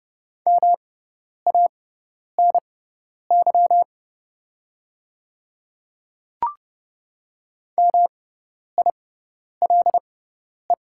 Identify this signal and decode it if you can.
Morse code, heard as a keyed tone, about 29 wpm, spaced as for 10 wpm, 700 Hz. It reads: MANY MILE
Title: 6x Character Spacing - Top 300 Words - 30wpm